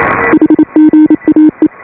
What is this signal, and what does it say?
Morse code, heard as a keyed tone, about 27 wpm, 310 Hz. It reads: HGAE